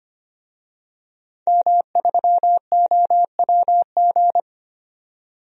Morse code, heard as a keyed tone, about 25 wpm, 700 Hz. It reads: M3OWG
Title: DXCC-100 Call Signs - 5 Characters - 25wpm